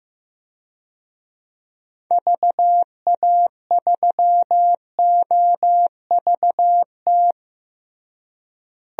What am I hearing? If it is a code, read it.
Morse code, heard as a keyed tone, about 16 wpm, 700 Hz. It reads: VA3OVT
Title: DXCC-100 Call Signs - 6 Characters - Repeated 3x - 15wpm